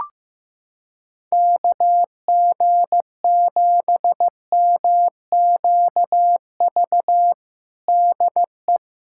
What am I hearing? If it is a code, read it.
Morse code, heard as a keyed tone, about 15 wpm, 700 Hz. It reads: KG7MQV DE
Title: USA Callsign DE Callsign 15wpm